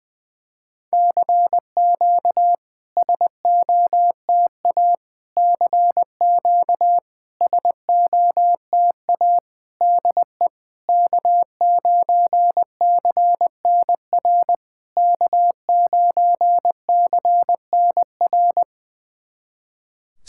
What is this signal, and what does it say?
Morse code, heard as a keyed tone, about 20 wpm, 690 Hz. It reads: CQ SOTA CQ SOTA DE K9CNR K9CNR
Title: SOTA QSO Practice 20wpm